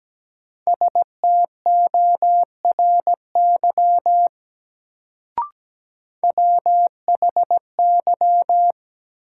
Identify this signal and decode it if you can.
Morse code, heard as a keyed tone, about 17 wpm, 700 Hz. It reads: STORY WHY
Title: Sets of 3 Words - New Word Drill - 17wpm